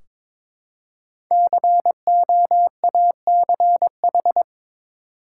Morse code, heard as a keyed tone, about 22 wpm, 700 Hz. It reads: COACH
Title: Five Letter Words - 22wpm